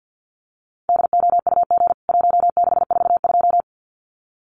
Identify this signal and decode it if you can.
Morse code, heard as a keyed tone, about 49 wpm, 700 Hz. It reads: DOUG 1642
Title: CWT Exchange 50wpm